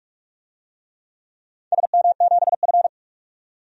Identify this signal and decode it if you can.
Morse code, heard as a keyed tone, about 46 wpm, 700 Hz. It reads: SM7F
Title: DXCC-100 Call Signs - 4 Characters - 45wpm